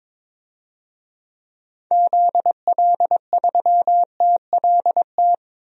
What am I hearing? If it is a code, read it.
Morse code, heard as a keyed tone, about 22 wpm, 700 Hz. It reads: ZL3TLT
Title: DXCC-100 Call Signs - 6 Characters - 22wpm